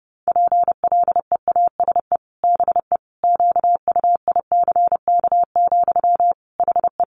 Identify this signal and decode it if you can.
Morse code, heard as a keyed tone, about 30 wpm, 700 Hz. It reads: PLEASE BE QUICK, HE
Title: Sentences from Top 1500 Words 30wpm